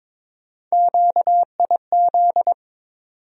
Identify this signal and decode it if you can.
Morse code, heard as a keyed tone, about 22 wpm, 700 Hz. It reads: QIZ
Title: USA Call Sign Postfixes - 3 Letters 22wpm